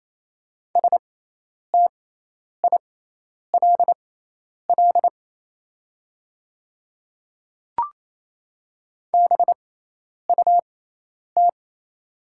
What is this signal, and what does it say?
Morse code, heard as a keyed tone, about 27 wpm, spaced as for 10 wpm, 700 Hz. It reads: STILL BUT
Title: 6x Character Spacing - Top 300 Words - 28wpm